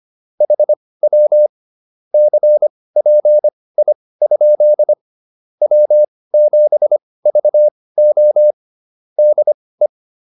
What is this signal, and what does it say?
Morse code, heard as a keyed tone, about 25 wpm, spaced as for 18 wpm, 600 Hz. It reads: HW CPI? W7VO DE
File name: QSO 105 18